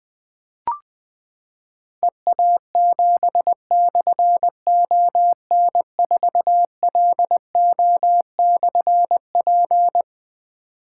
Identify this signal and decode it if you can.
Morse code, heard as a keyed tone, about 20 wpm, 700 Hz. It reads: EA7/ON4LO/P
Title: Difficult Call Signs 20wpm